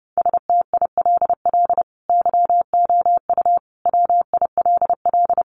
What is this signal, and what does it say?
Morse code, heard as a keyed tone, about 30 wpm, 700 Hz. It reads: STILL YOU WILL
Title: Sentences from Top 600 Words 30wpm